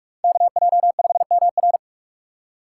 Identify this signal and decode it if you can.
Morse code, heard as a keyed tone, about 44 wpm, 700 Hz. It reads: KJ5MR